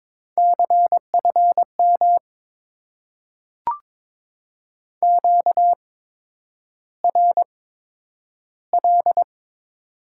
Morse code, heard as a keyed tone, about 22 wpm, spaced as for 9 wpm, 700 Hz. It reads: CFM QRL
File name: Top 90 QSO Elements - 8x Character Spacing - 22wpm